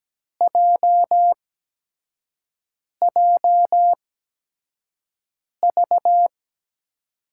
Morse code, heard as a keyed tone, about 17 wpm, 700 Hz. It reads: J J V
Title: USA Call Signs - 8x Character Spacing - 17wpm